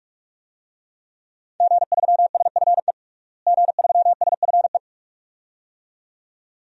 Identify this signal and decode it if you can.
Morse code, heard as a keyed tone, about 46 wpm, 700 Hz. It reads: G3SFE G3SFE
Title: DXCC-100 Call Signs - 5 Characters - Repeated 2x - 45wpm